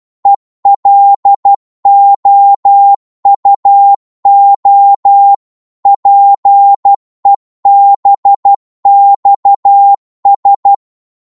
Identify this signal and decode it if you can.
Morse code, heard as a keyed tone, about 12 wpm, 800 Hz. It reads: ELOUO PEBXS
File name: Morse / ltr-12wpm